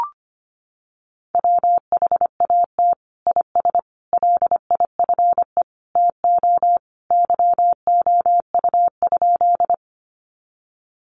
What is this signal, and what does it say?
Morse code, heard as a keyed tone, about 25 wpm, 700 Hz. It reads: WHAT IS LIFE TO YOU?
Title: Sentences from Top 300 Words 25wpm